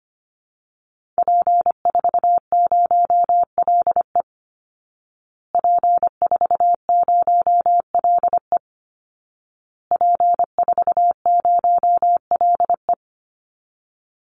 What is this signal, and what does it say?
Morse code, heard as a keyed tone, about 25 wpm, 700 Hz. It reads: P40LE P40LE P40LE
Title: Call Signs Repeated 2x - 25wpm